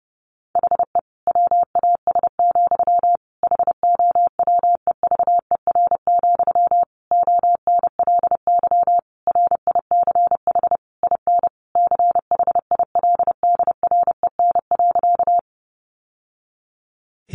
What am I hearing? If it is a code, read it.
Morse code, heard as a keyed tone, about 30 wpm, 700 Hz. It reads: HE WAS, HOWEVER, ONLY RICH IN CHILDREN.
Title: Sentences from Top 1100 Words 30wpm